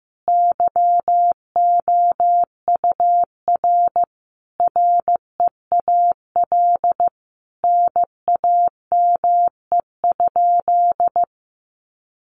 Morse code, heard as a keyed tone, about 15 wpm, 700 Hz. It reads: YOUR REAL NAME?
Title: Sentences from Top 300 Words 15wpm